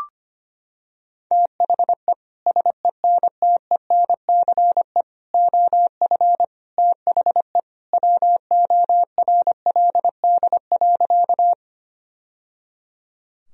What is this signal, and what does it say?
Morse code, heard as a keyed tone, about 25 wpm, 700 Hz. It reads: THE SENTENCE OF THE WORLD.
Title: Sentences from Top 1000 Words - Review Part 1 - 25wpm